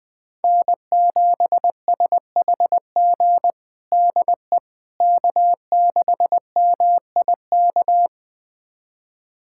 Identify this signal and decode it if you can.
Morse code, heard as a keyed tone, about 20 wpm, 700 Hz. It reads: N7SHG DE K6MIK
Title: USA Callsign DE Callsign - 5 Characters - 20wpm